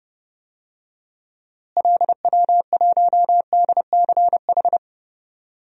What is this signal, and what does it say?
Morse code, heard as a keyed tone, about 30 wpm, 700 Hz. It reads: LW1DCH
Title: DXCC-100 Call Signs - 6 Characters - Repeated 2x - 30wpm